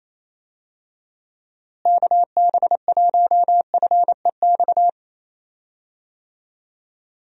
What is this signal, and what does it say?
Morse code, heard as a keyed tone, about 27 wpm, 700 Hz. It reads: KB1FEX
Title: USA Call Signs - 6 Characters - Repeated 2x - 28wpm